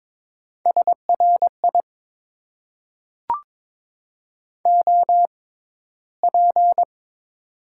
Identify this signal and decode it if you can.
Morse code, heard as a keyed tone, about 22 wpm, spaced as for 13 wpm, 700 Hz. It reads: SRI OP
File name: Top 90 QSO Elements - 6x Character Spacing - 22wpm